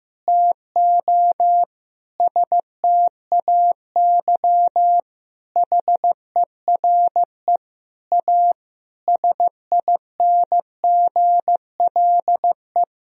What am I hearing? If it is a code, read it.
Morse code, heard as a keyed tone, about 15 wpm, 700 Hz. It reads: TO STAY HERE A SINGLE
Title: Sentences from Top 1000 Words - Review Part 4 - 15wpm